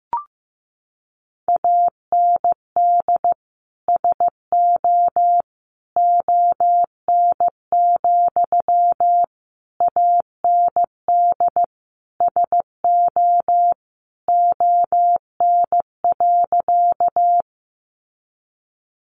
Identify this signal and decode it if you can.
Morse code, heard as a keyed tone, about 15 wpm, 700 Hz. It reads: AND SO ON, AND SO ON.
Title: Sentences from Top 100 Words 15wpm